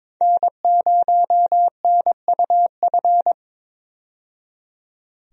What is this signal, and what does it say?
Morse code, heard as a keyed tone, about 22 wpm, 700 Hz. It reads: N0NUF
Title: USA Call Signs - 5 Characters - 22wpm